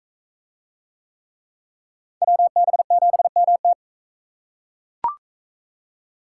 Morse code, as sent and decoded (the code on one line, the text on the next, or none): .-- -... --... -- -
WB7MT